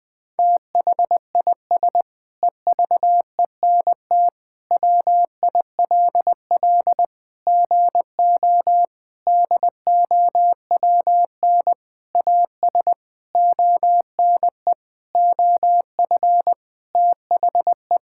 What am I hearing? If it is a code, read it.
Morse code, heard as a keyed tone, about 20 wpm, 700 Hz. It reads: THIS EVENT WILL GO DOWN AS ONE OF THE
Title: Sentences with 7-Word Phrases - 20wpm